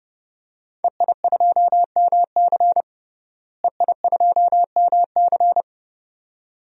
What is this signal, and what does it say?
Morse code, heard as a keyed tone, about 30 wpm, 700 Hz. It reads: EI2MC EI2MC
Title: DXCC-100 Call Signs - 5 Characters - Repeated 2x - 30wpm